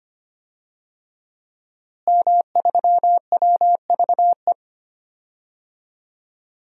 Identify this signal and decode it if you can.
Morse code, heard as a keyed tone, about 25 wpm, 700 Hz. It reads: M3WVE
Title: DXCC-100 Call Signs - 5 Characters - Repeated 3x - 25wpm